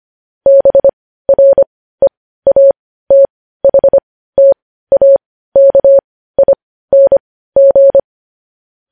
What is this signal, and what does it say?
Morse code, heard as a keyed tone, about 25 wpm, spaced as for 15 wpm, 550 Hz. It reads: BREATHTAKING